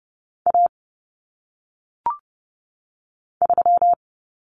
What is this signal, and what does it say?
Morse code, heard as a keyed tone, about 30 wpm, 700 Hz. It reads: A 3
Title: Common Cut Numbers - 1-Digit 30wpm